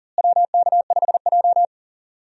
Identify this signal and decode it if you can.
Morse code, heard as a keyed tone, about 40 wpm, 700 Hz. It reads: WK5J
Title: USA Call Signs - 4 Characters - Repeated 3x - 40wpm